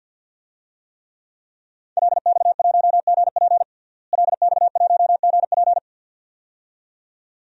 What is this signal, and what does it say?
Morse code, heard as a keyed tone, about 51 wpm, 700 Hz. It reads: LX1GP LX1GP
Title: DXCC-100 Call Signs - 5 Characters - Repeated 2x - 50wpm